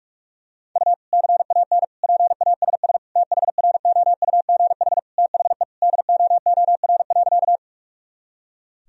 Morse code, heard as a keyed tone, about 46 wpm, 700 Hz. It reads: U CAN PASS THROUGH THE DOOR.
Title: Sentences from Top 400 Words and Abbreviations 45wpm